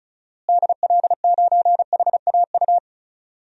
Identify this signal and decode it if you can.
Morse code, heard as a keyed tone, about 34 wpm, 700 Hz. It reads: DL9HAU
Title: DXCC-100 Call Signs - 6 Characters - Repeated 3x - 35wpm